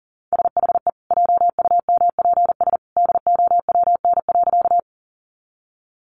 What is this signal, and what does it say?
Morse code, heard as a keyed tone, about 40 wpm, 700 Hz. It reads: SHE JUMPS DOWN.
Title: Easy Phrases 40wpm